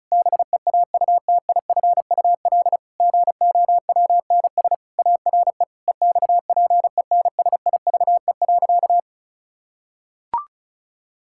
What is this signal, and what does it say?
Morse code, heard as a keyed tone, about 36 wpm, 700 Hz. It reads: BEAUTIFUL GOWNS ARE EXPENSIVE.
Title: Pangram Challenge v2 - 35wpm